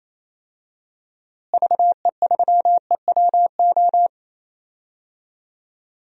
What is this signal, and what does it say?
Morse code, heard as a keyed tone, about 28 wpm, 700 Hz. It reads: VE3EWO
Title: DXCC-100 Call Signs - 6 Characters - Repeated 3x - 28wpm